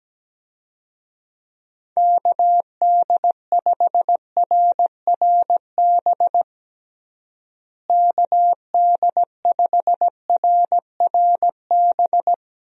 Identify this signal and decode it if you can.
Morse code, heard as a keyed tone, about 17 wpm, 700 Hz. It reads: KD5RRB KD5RRB
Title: USA Call Signs - 6 Characters - Repeated 2x - 17wpm